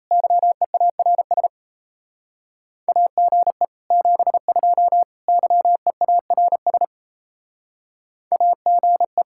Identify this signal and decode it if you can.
Morse code, heard as a keyed tone, about 34 wpm, 700 Hz. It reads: YEARS AGE 72 YEARS AGE
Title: Speed-Racing Common QSO Phrases Encore - 28wpm